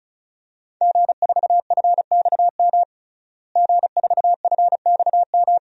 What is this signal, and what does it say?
Morse code, heard as a keyed tone, about 35 wpm, 700 Hz. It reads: G4FXM G4FXM
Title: DXCC-100 Call Signs - 5 Characters - Repeated 2x - 35wpm